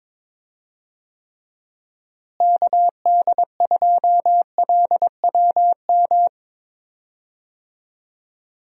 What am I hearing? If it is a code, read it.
Morse code, heard as a keyed tone, about 22 wpm, 700 Hz. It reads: KD2LWM
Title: USA Call Signs - 6 Characters - Repeated 3x - 22wpm